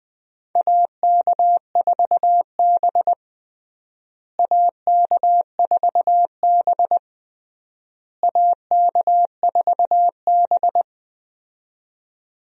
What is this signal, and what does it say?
Morse code, heard as a keyed tone, about 20 wpm, 700 Hz. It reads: AK4B AK4B AK4B
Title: USA Call Signs - 4 Characters - Repeated 3x - 20wpm